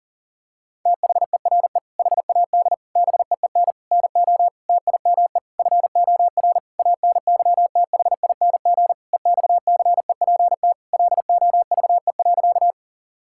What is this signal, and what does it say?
Morse code, heard as a keyed tone, about 40 wpm, 700 Hz. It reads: THERE HAD BEEN NO TIME FOR ANYTHING EXCEPT LOVE.
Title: Sentences from Top 1600 Words 40wpm